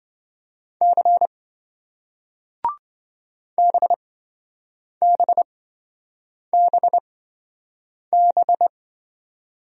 Morse code, heard as a keyed tone, about 26 wpm, spaced as for 7 wpm, 700 Hz. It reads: C BBBB